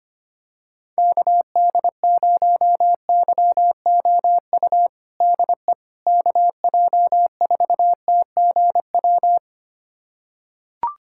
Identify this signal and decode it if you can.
Morse code, heard as a keyed tone, about 25 wpm, 700 Hz. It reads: KD0YOU DE KJ4TGW
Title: USA Callsign DE Callsign - 6 Characters - 25wpm